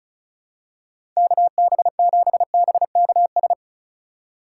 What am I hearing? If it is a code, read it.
Morse code, heard as a keyed tone, about 34 wpm, 700 Hz. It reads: KB7BKS
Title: USA Call Signs - 6 Characters - 4x Character Spacing - 35wpm